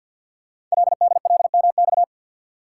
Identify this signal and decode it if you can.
Morse code, heard as a keyed tone, about 48 wpm, 700 Hz. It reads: LDLMX
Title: Code Groups - 5 Letters - 50wpm